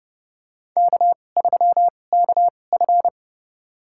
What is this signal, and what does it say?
Morse code, heard as a keyed tone, about 30 wpm, spaced as for 22 wpm, 700 Hz. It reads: K3KF
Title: USA Call Signs - 4 Characters - 2x Character Spacing - 30wpm